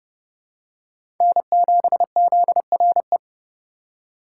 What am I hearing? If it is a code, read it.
Morse code, heard as a keyed tone, about 30 wpm, 700 Hz. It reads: N7ZRE
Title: USA Call Signs - 5 Characters - 3x Character Spacing - 30wpm